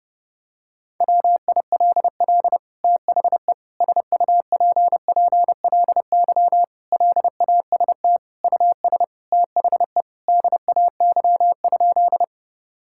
Morse code, heard as a keyed tone, about 30 wpm, 700 Hz. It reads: WILL THE SUPPLY LAST US THE DAY?
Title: Sentences from Top 1000 Words - Review Part 5 - 30wpm